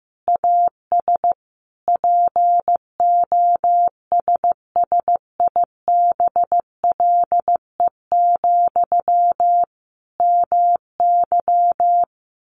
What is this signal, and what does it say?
Morse code, heard as a keyed tone, about 15 wpm, 700 Hz. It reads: AS POSSIBLE, MY
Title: Sentences from Top 800 Words 15wpm